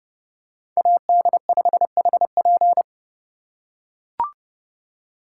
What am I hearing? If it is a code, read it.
Morse code, heard as a keyed tone, about 31 wpm, 700 Hz. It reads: AD5HP